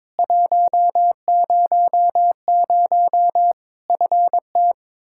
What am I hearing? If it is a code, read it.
Morse code, heard as a keyed tone, about 22 wpm, 700 Hz. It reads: 100 FT
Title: Practice QSO - Exchange 2 - Info - 22wpm